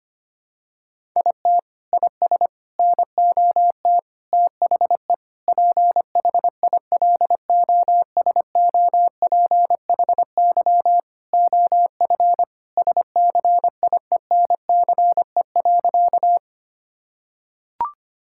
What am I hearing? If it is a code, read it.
Morse code, heard as a keyed tone, about 25 wpm, 700 Hz. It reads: IT IS NOT THE PHILOSOPHY OF SCIENCE.
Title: Sentences from Top 1000 Words - Review Part 6 - 25wpm